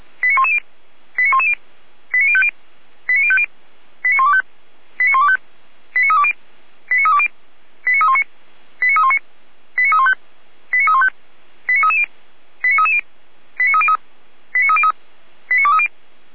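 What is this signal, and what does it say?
Five-tone selective call (ZVEI), heard as quick runs of five tones, tones of about 70 ms. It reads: 89110 89110 89050 89050 89125 89125 89320 89320 89319 89319 89315 89315 89330 89330 89303 89303 89230
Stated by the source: ZVEI five-tone selective call received by SDR.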